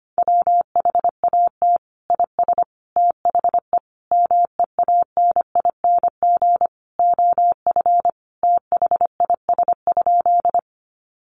Morse code, heard as a keyed tone, about 25 wpm, 700 Hz. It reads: WHAT IS THE MEANING OF THIS?
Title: Sentences from Top 900 Words 25wpm